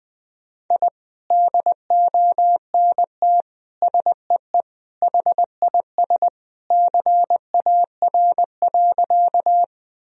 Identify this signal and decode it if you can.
Morse code, heard as a keyed tone, about 21 wpm, 700 Hz. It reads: I DONT SEE HIS CAR.
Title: Sentences from Top 1000 Words - Review Part 1 - 20wpm